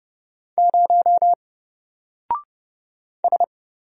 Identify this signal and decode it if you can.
Morse code, heard as a keyed tone, about 30 wpm, 700 Hz. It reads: 0 S
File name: Single-Letter-Number-30wpm